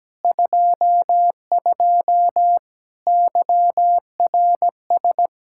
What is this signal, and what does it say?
Morse code, heard as a keyed tone, about 17 wpm, 700 Hz. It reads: 22 YRS
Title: Practice QSO - Exchange 3 - More Info - 17wpm